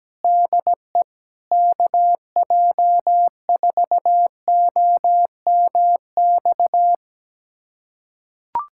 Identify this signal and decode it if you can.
Morse code, heard as a keyed tone, about 17 wpm, 700 Hz. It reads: DE KJ4OMX
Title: USA Callsign DE Callsign - 6 Characters - 17wpm